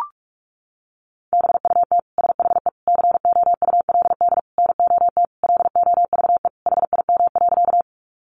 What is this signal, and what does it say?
Morse code, heard as a keyed tone, about 44 wpm, 700 Hz. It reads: BUT SHE COULD NOT LOVE HIM.
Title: Sentences from Top 600 Words 45wpm